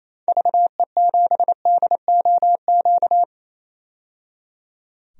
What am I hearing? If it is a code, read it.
Morse code, heard as a keyed tone, about 28 wpm, 700 Hz. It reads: VE7DOQ
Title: DXCC-100 Call Signs - 6 Characters - 28wpm